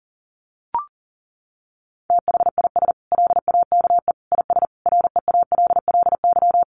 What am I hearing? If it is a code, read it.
Morse code, heard as a keyed tone, about 40 wpm, 700 Hz. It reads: THIS LAKE IS REALLY
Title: Sentences from Top 1100 Words 40wpm